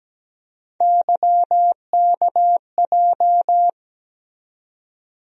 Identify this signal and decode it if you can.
Morse code, heard as a keyed tone, about 17 wpm, 700 Hz. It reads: YKJ
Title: USA Call Sign Postfixes - 3 Letters 17wpm